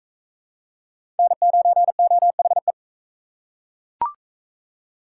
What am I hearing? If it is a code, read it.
Morse code, heard as a keyed tone, about 37 wpm, 700 Hz. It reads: N9OHE